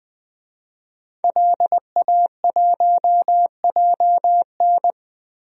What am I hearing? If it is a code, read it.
Morse code, heard as a keyed tone, about 20 wpm, 700 Hz. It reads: LA1JN
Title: DXCC-100 Call Signs - 5 Characters - 20wpm